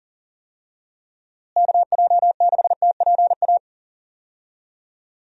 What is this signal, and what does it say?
Morse code, heard as a keyed tone, about 40 wpm, 700 Hz. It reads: KJ6TPA
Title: USA Call Signs - 6 Characters - Repeated 2x - 40wpm